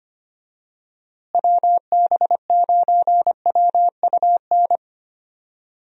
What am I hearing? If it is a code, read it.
Morse code, heard as a keyed tone, about 25 wpm, 700 Hz. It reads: WB9WUN
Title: USA Call Signs - 6 Characters - Repeated 2x - 25wpm